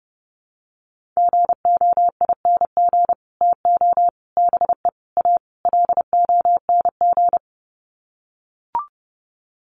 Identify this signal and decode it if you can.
Morse code, heard as a keyed tone, about 30 wpm, 700 Hz. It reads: GOING TO BE A LONG
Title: Sets of 5 Words Encore2 30wpm